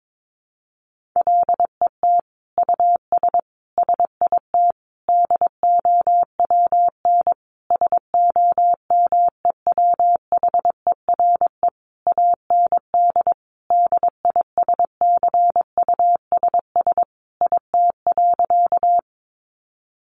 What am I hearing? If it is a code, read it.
Morse code, heard as a keyed tone, about 22 wpm, 700 Hz. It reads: LET US SIT DOWN SOMEWHERE AND DISCUSS IT.
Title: Sentences from Top 1400 Words 22wpm